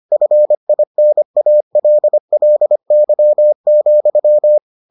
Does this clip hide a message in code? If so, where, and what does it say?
Yes: Morse code from 0.1 to 4.6 s, heard as a keyed tone, about 25 wpm, 600 Hz. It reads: FINALLY,